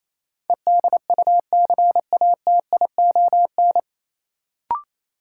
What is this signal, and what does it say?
Morse code, heard as a keyed tone, about 28 wpm, 700 Hz. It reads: EDUCATION
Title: Sentences from Top 1000 Words - Review Part 6 - New Word Drill - 28wpm